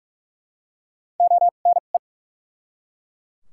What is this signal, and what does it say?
Morse code, heard as a keyed tone, about 43 wpm, spaced as for 32 wpm, 700 Hz. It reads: ONE
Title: 2x Character Spacing - Top 100 Words - 45wpm